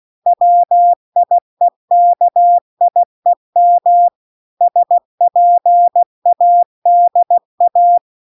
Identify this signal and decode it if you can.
Morse code, heard as a keyed tone, about 16 wpm, 700 Hz. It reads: WIEKIEM SPADA